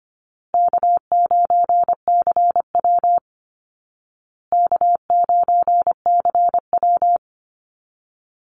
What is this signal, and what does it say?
Morse code, heard as a keyed tone, about 25 wpm, 700 Hz. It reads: K9CW K9CW
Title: Easy Call Signs Repeated 3x - 25wpm